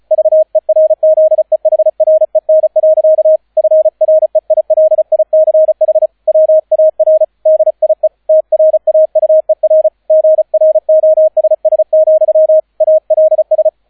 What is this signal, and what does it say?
Morse code, heard as a keyed tone, about 35 wpm, 630 Hz, with radio noise. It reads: VERZEHREN. FREILICH WAR DIE TRAUER GROSS, ALS